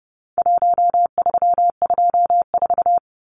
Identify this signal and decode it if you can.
Morse code, heard as a keyed tone, about 30 wpm, 700 Hz. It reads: 1324
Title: Numbers - Contest Style - 4-Digit 30wpm